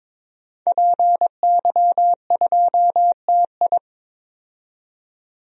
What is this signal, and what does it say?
Morse code, heard as a keyed tone, about 22 wpm, 700 Hz. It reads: PY2TI
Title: DXCC-100 Call Signs - 5 Characters - 22wpm